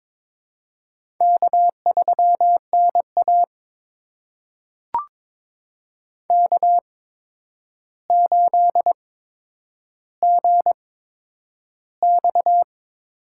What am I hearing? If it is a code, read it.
Morse code, heard as a keyed tone, about 22 wpm, spaced as for 9 wpm, 700 Hz. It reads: K3NA K8GX
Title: USA Call Signs - 8x Character Spacing - 22wpm